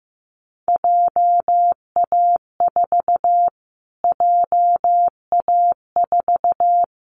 Morse code, heard as a keyed tone, about 15 wpm, 700 Hz. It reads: JA4 JA4
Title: Callsign Prefixes for Top 25 DXCC - Repeated 3x - 15wpm